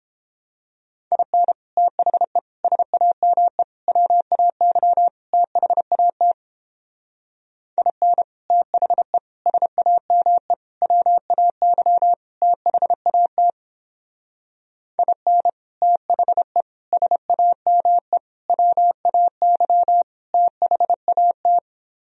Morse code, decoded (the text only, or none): IN THE SAME WAY THAT IN THE SAME WAY THAT IN THE SAME WAY THAT